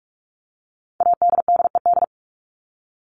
Morse code, heard as a keyed tone, about 44 wpm, 700 Hz. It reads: ADDED
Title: ED Suffix - Top 100 Words with Base Words 45wpm